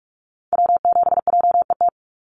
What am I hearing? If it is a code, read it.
Morse code, heard as a keyed tone, about 44 wpm, 700 Hz. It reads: W7JET
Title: NA SOTA Activator Callsigns 45wpm